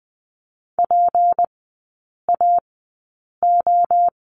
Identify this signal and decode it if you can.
Morse code, heard as a keyed tone, about 20 wpm, spaced as for 8 wpm, 700 Hz. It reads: PAO
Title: 184 Lesson 30e - Letter b - Three Characters 20wpm